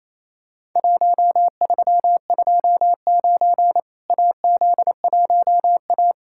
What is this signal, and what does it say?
Morse code, heard as a keyed tone, about 28 wpm, 700 Hz. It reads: 1329 AZ1A
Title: Worked All Europe DX Contest - QTC Exchange - 28wpm